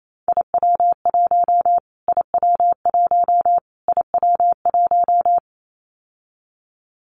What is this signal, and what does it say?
Morse code, heard as a keyed tone, about 28 wpm, 700 Hz. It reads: IW1 IW1 IW1
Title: Callsign Prefixes for Top 25 DXCC - Repeated 3x - 28wpm